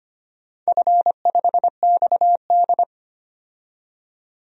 Morse code, heard as a keyed tone, about 25 wpm, 700 Hz. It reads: F5XD
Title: DXCC-100 Call Signs - 4 Characters - 25wpm